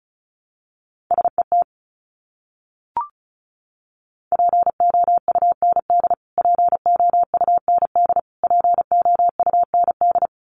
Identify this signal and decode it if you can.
Morse code, heard as a keyed tone, about 34 wpm, 700 Hz. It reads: SET POUND POUND POUND
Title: Top 500 Words Repeated 35wpm